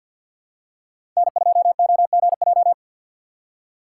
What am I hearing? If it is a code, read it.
Morse code, heard as a keyed tone, about 46 wpm, 700 Hz. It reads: N2OGJ
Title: USA Call Signs - 3x Character Spacing - 50wpm